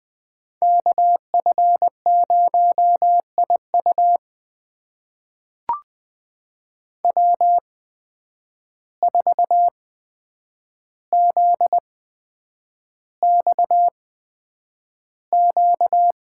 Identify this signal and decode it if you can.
Morse code, heard as a keyed tone, about 20 wpm, spaced as for 9 wpm, 700 Hz. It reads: KF0IU W4ZXQ
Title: USA Call Signs - 5 Characters - 8x Character Spacing - 20wpm